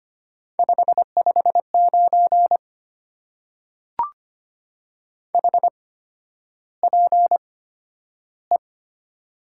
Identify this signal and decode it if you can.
Morse code, heard as a keyed tone, about 25 wpm, spaced as for 11 wpm, 700 Hz. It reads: 559 HPE
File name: Top 90 QSO Elements - 8x Character Spacing - 25wpm